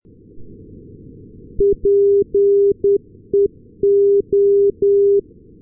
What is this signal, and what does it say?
Morse code, heard as a keyed tone, about 10 wpm, 400 Hz. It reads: PEO
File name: europe / s-280 ndb nykoping-peola peo-398, sm1otx